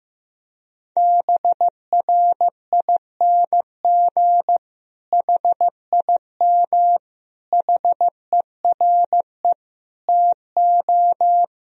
Morse code, heard as a keyed tone, about 15 wpm, 700 Hz. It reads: BRING HIM HERE TO
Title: Sentences from Top 1000 Words - Review Part 3 - 15wpm